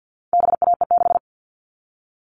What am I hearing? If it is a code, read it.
Morse code, heard as a keyed tone, about 50 wpm, 700 Hz. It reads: BAE6
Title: All Cut Numbers - 4-Digit 50wpm